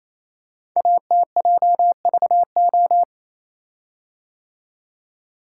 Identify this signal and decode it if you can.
Morse code, heard as a keyed tone, about 28 wpm, 700 Hz. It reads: ATJVO
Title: Code Groups - 5 Letters - 28wpm